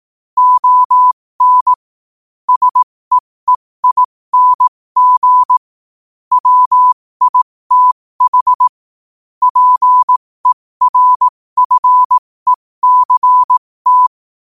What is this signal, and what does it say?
Morse code, heard as a keyed tone, about 18 wpm, spaced as for 15 wpm, 1000 Hz. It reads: ON SEEING WITH PERFECT
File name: qotdfiles / 15wpm